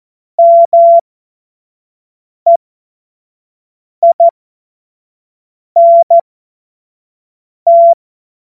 Morse code, heard as a keyed tone, about 14 wpm, spaced as for 5 wpm, 680 Hz. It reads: MEINT
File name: Morsekurs / Witze / Witz 25. Januar 2022, 25 F70